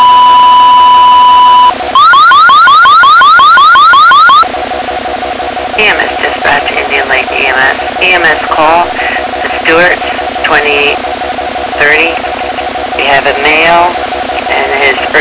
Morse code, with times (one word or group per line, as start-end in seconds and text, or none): none